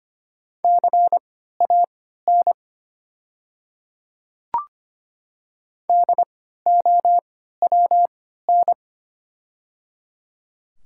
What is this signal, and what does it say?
Morse code, heard as a keyed tone, about 25 wpm, spaced as for 14 wpm, 700 Hz. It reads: CAN DOWN
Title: 3x Character Spacing - Top 500 Words - 25wpm